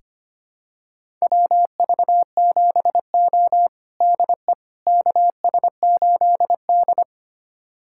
Morse code, heard as a keyed tone, about 25 wpm, 700 Hz. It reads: WV7O DE KS8D
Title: USA Callsign DE Callsign - 4 Characters - 25wpm